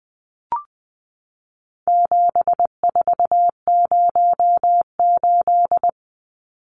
Morse code, heard as a keyed tone, about 20 wpm, 700 Hz. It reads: 7408
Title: Numbers - 4-Digit 20wpm